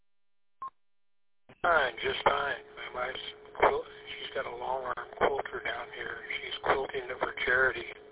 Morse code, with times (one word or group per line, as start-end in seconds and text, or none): none